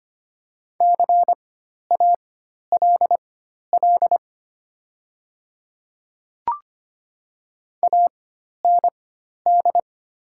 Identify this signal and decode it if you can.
Morse code, heard as a keyed tone, about 25 wpm, spaced as for 12 wpm, 700 Hz. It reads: CALL AND
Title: 4x Character Spacing - Top 100 Words - 25wpm